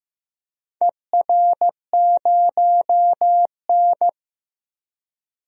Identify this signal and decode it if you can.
Morse code, heard as a keyed tone, about 15 wpm, 700 Hz. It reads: ER0N
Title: DXCC-100 Call Signs - 4 Characters - 15wpm